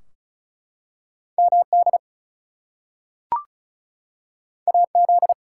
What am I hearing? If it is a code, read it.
Morse code, heard as a keyed tone, about 36 wpm, 700 Hz. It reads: MD AZ
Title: US States and Canadian Provinces 35wpm